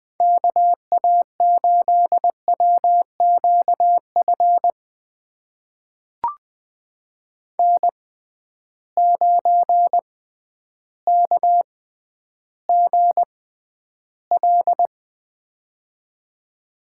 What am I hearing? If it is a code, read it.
Morse code, heard as a keyed tone, about 21 wpm, spaced as for 11 wpm, 700 Hz. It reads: KA8WQF N9KGL
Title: USA Call Signs - 6x Character Spacing - 20wpm